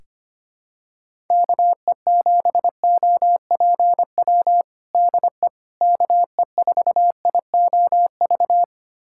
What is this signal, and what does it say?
Morse code, heard as a keyed tone, about 25 wpm, 700 Hz. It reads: KE7OPW DE KE4IOV
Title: USA Callsign DE Callsign - 6 Characters - 25wpm